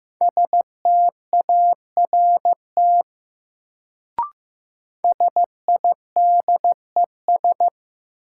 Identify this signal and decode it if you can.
Morse code, heard as a keyed tone, about 15 wpm, 700 Hz. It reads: START SIDES
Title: Sets of 4 Words - New Word Drill - 15wpm